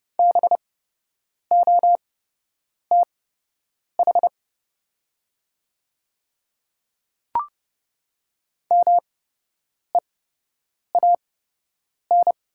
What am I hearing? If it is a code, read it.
Morse code, heard as a keyed tone, about 30 wpm, spaced as for 8 wpm, 700 Hz. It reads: BOTH MEAN
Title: 8x Character Spacing - Top 500 Words - 30wpm